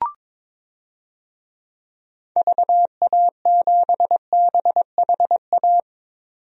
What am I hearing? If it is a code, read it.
Morse code, heard as a keyed tone, about 21 wpm, 700 Hz. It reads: VA7BHA
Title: DXCC-100 Call Signs - 6 Characters - 22wpm